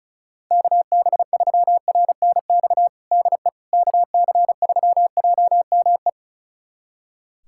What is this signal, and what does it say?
Morse code, heard as a keyed tone, about 34 wpm, 700 Hz. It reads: KB3RNX DE KC3JME
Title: USA Callsign DE Callsign 35wpm